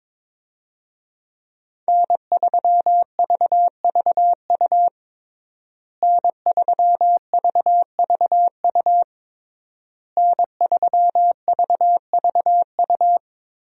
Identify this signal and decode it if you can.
Morse code, heard as a keyed tone, about 21 wpm, 700 Hz. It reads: N3VVU N3VVU N3VVU
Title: USA Call Signs - 5 Characters - Repeated 3x - 22wpm